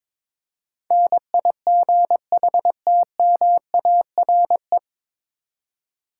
Morse code, heard as a keyed tone, about 22 wpm, 700 Hz. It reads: NIGHTMARE
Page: Morse Code Ninja practice 22wpm